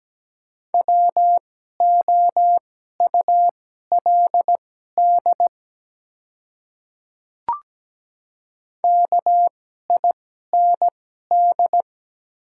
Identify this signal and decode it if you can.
Morse code, heard as a keyed tone, about 17 wpm, spaced as for 12 wpm, 700 Hz. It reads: WOULD KIND
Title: 2x Character Spacing - Top 300 Words - 17wpm